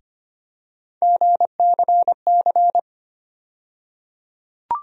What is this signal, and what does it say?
Morse code, heard as a keyed tone, about 25 wpm, 700 Hz. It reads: GCC
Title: USA Call Sign Postfixes - 3 Letters 25wpm